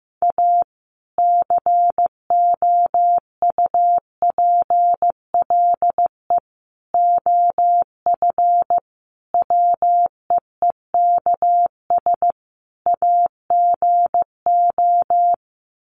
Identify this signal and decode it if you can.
Morse code, heard as a keyed tone, about 15 wpm, 700 Hz. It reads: A COUPLE OF WEEKS AGO
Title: Sets of 5 Words 15wpm